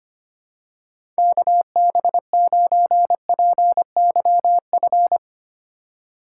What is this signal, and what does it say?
Morse code, heard as a keyed tone, about 25 wpm, 700 Hz. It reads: KB9PYF
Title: USA Call Signs - 6 Characters - 3x Character Spacing - 25wpm